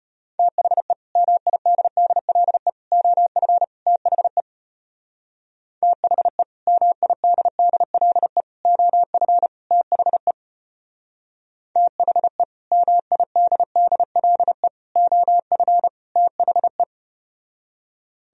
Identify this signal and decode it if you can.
Morse code, heard as a keyed tone, about 34 wpm, 700 Hz. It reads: THE MIDDLE OF THE THE MIDDLE OF THE THE MIDDLE OF THE